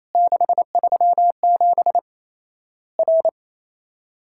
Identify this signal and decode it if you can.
Morse code, heard as a keyed tone, about 28 wpm, 700 Hz. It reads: 637 R
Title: Worked All Europe DX Contest - QTC Exchange - 28wpm